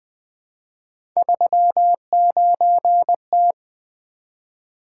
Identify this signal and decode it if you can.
Morse code, heard as a keyed tone, about 20 wpm, 700 Hz. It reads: 39T
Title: Three Letter-Number Combinations - 20wpm